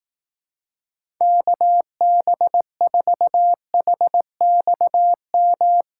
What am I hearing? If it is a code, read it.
Morse code, heard as a keyed tone, about 18 wpm, 700 Hz. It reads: KB4HXM
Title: Speed Racing - USA 2x3 Call Signs 15wpm